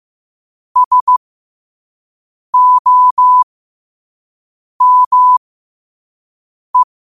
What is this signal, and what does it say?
Morse code, heard as a keyed tone, about 15 wpm, spaced as for 5 wpm, 1000 Hz. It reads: SOME
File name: qotdfiles / 5wpm